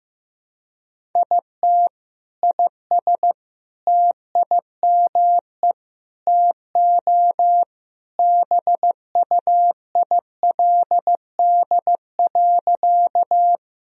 Morse code, heard as a keyed tone, about 15 wpm, 700 Hz. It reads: IT IS TIME TO BUILD.
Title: Sentences from Top 1000 Words - Review Part 1 - 15wpm